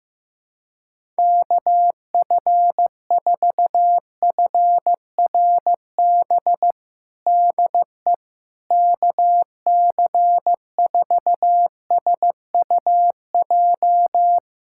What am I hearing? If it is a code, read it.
Morse code, heard as a keyed tone, about 16 wpm, 700 Hz. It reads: KF4FRB DE KC4SUJ
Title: USA Callsign DE Callsign - 6 Characters - 15wpm